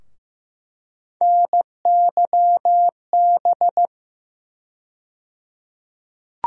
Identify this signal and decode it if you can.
Morse code, heard as a keyed tone, about 15 wpm, 700 Hz. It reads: NYB
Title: USA Call Sign Postfixes - 3 Letters 15wpm